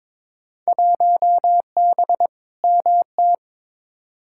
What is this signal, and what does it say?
Morse code, heard as a keyed tone, about 23 wpm, 700 Hz. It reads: 1B MT
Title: Field Day Exchanges - 22wpm